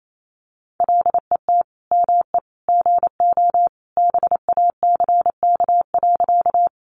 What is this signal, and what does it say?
Morse code, heard as a keyed tone, about 28 wpm, 700 Hz. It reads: LET ME GO BACK.
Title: Sentences from Top 300 Words 28wpm